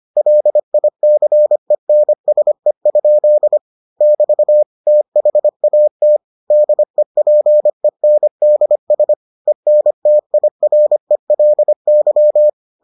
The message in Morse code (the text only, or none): LICENSE? <BT> THAT DEPENDS ENTIRELY